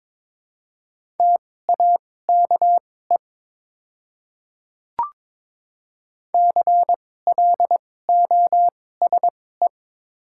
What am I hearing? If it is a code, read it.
Morse code, heard as a keyed tone, about 22 wpm, spaced as for 16 wpm, 700 Hz. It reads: TAKE CLOSE
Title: 2x Character Spacing - Top 500 Words - 22wpm